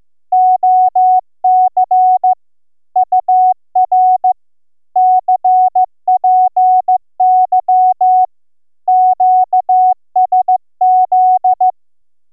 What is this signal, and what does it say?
Morse code, heard as a keyed tone, about 14 wpm, 740 Hz. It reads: OC UR CPY QSZ